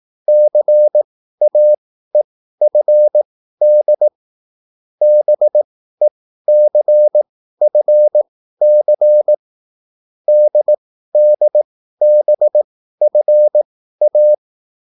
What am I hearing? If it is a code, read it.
Morse code, heard as a keyed tone, about 18 wpm, spaced as for 13 wpm, 600 Hz. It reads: CAEFD BECFC DDBFA
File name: QSL / CW lesson